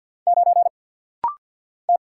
Morse code, heard as a keyed tone, about 49 wpm, 700 Hz. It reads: 9 T
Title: Single Letter-Number - Rapid-Fire 50wpm